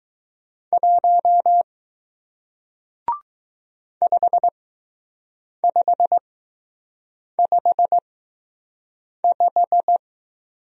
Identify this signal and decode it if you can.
Morse code, heard as a keyed tone, about 20 wpm, spaced as for 6 wpm, 700 Hz. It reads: 1 5555